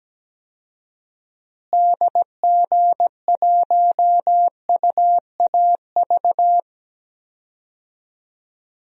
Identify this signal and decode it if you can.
Morse code, heard as a keyed tone, about 17 wpm, 700 Hz. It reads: DG1UAV
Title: DXCC-100 Call Signs - 6 Characters - Repeated 2x - 17wpm